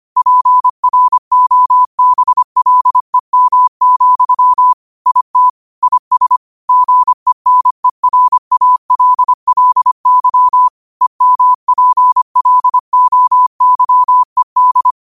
Morse code, heard as a keyed tone, about 25 wpm, 1000 Hz. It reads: PROBLEM, IT IS GENERALLY EMPLOYED
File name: qotdfiles / 25wpm